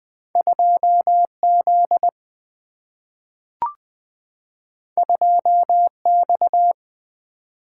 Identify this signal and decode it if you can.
Morse code, heard as a keyed tone, about 20 wpm, 700 Hz. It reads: 2Z 2X
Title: Single-Letter Single-Number 20wpm